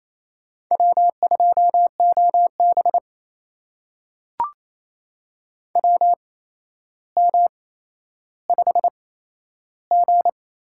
Morse code, heard as a keyed tone, about 28 wpm, spaced as for 12 wpm, 700 Hz. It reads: W2OB WM5G
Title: USA Call Signs - 4 Characters - 8x Character Spacing - 28wpm